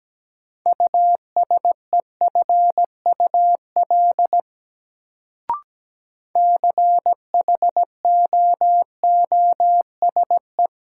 Morse code, heard as a keyed tone, about 17 wpm, 700 Hz. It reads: USEFUL CHOOSE
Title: Sentences from Top 700 Words - New Word Drill - 17wpm